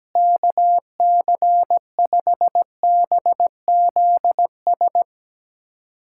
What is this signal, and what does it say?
Morse code, heard as a keyed tone, about 17 wpm, 700 Hz. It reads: KC5BZS
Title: USA Callsign DE Callsign 17wpm